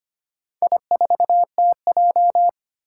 Morse code, heard as a keyed tone, about 25 wpm, 700 Hz. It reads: I4TJ
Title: Four Challenging Letter-Number Combinations 25wpm